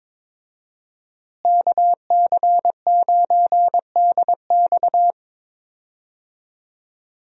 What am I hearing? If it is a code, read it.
Morse code, heard as a keyed tone, about 22 wpm, 700 Hz. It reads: KC9DX
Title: USA Call Signs - 5 Characters - Repeated 2x - 22wpm